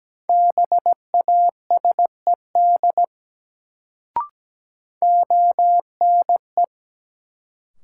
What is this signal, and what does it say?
Morse code, heard as a keyed tone, about 17 wpm, 700 Hz. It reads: BASED ONE
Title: Sets of 2 Words - New Word Drill - 17wpm